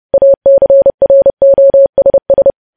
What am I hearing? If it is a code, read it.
Morse code, heard as a keyed tone, about 30 wpm, 550 Hz. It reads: ACROSS
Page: news headlines in morse code at 30 wpm